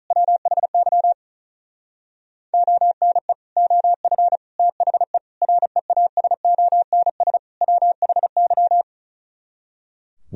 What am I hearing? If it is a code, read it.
Morse code, heard as a keyed tone, about 45 wpm, 700 Hz. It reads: WHY ONE OF THE REASONS WHY